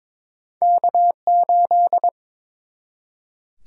Morse code, heard as a keyed tone, about 22 wpm, 700 Hz. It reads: K8
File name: practice / 22wpm / Two Letter-Number Combinations - 22wpm